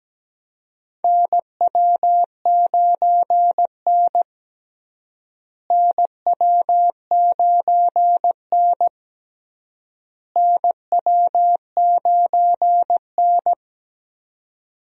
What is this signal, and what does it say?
Morse code, heard as a keyed tone, about 17 wpm, 700 Hz. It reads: NW9N NW9N NW9N
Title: USA Call Signs - 4 Characters - Repeated 3x - 17wpm